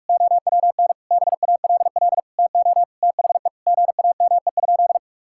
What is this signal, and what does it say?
Morse code, heard as a keyed tone, about 41 wpm, 700 Hz. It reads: OWN BALL TO THE GAME?
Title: Sentences from Top 1000 Words - Review Part 6 - 45wpm